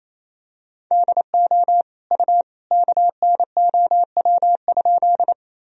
Morse code, heard as a keyed tone, about 28 wpm, 700 Hz. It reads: DO U KNOW?
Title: Sentences from Top 100 Words and Abbreviations 28wpm